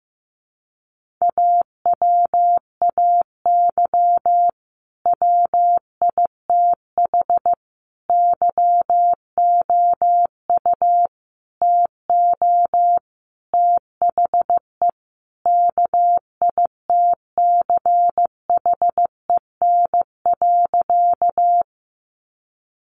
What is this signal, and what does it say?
Morse code, heard as a keyed tone, about 15 wpm, 700 Hz. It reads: AWAY WITH YOU TO THE KITCHEN.
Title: Sentences from Top 1400 Words 15wpm